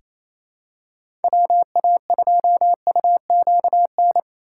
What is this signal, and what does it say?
Morse code, heard as a keyed tone, about 28 wpm, 700 Hz. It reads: WA2UQN
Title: USA Call Signs - 6 Characters - 2x Character Spacing - 28wpm